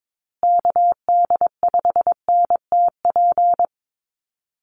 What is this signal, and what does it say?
Morse code, heard as a keyed tone, about 22 wpm, 700 Hz. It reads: KD5NTP
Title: USA Call Signs Repeated 2x 22wpm